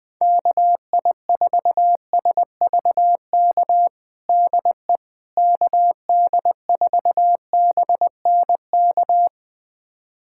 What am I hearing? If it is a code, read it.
Morse code, heard as a keyed tone, about 21 wpm, 700 Hz. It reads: KI4SVK DE KD4BNK
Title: USA Callsign DE Callsign - 6 Characters - 20wpm